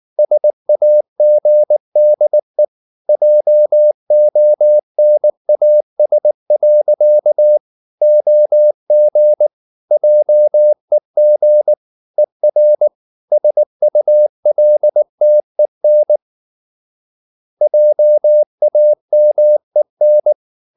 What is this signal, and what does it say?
Morse code, heard as a keyed tone, about 19 wpm, 600 Hz. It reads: SAGDE JONAS. OG JEG ER SULTEN JAMEN
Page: Ferie 19 wpm